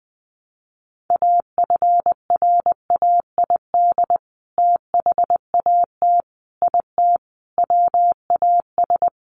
Morse code, heard as a keyed tone, about 20 wpm, 700 Hz. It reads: AFRAID THAT IT WAS
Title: Sentences from Top 1500 Words 20wpm